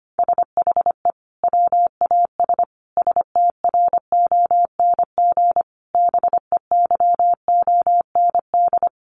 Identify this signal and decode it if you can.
Morse code, heard as a keyed tone, about 25 wpm, 700 Hz. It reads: SHE WAS STRONG BEYOND
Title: Sentences from Top 1300 Words 25wpm